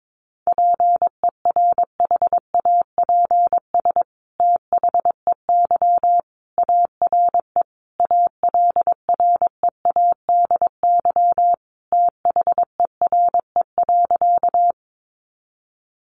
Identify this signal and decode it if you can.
Morse code, heard as a keyed tone, about 22 wpm, 700 Hz. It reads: PERHAPS THEY ARE ALREADY THERE.
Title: Sentences from Top 700 Words 22wpm